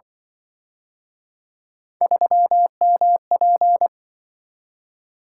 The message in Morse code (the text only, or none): 3MP